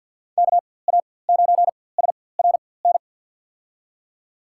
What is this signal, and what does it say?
Morse code, heard as a keyed tone, about 53 wpm, spaced as for 24 wpm, 700 Hz. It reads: KA9SRN
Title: USA Call Signs - 6 Characters - 4x Character Spacing - 50wpm